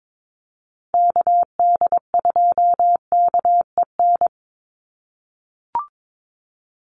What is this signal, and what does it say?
Morse code, heard as a keyed tone, about 22 wpm, 700 Hz. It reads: KD2KEN
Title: USA Call Signs 22wpm